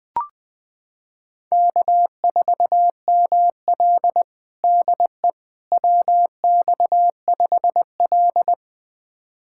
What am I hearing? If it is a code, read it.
Morse code, heard as a keyed tone, about 21 wpm, 700 Hz. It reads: K4ML DE WX5L
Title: USA Callsign DE Callsign - 4 Characters - 20wpm